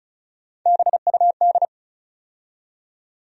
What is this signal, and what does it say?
Morse code, heard as a keyed tone, about 36 wpm, 700 Hz. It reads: BUD
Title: USA Call Sign Postfixes - 3 Letters 35wpm